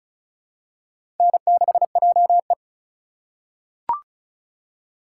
Morse code, heard as a keyed tone, about 36 wpm, 700 Hz. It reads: N6JE
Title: USA Call Signs - 4 Characters - 3x Character Spacing - 35wpm